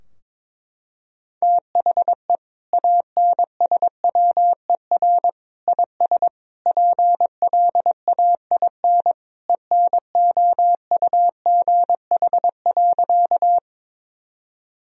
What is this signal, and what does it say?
Morse code, heard as a keyed tone, about 22 wpm, 700 Hz. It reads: THE ANSWER IS PLAIN ENOUGH.
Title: Sentences from Top 1000 Words - Review Part 2 - 22wpm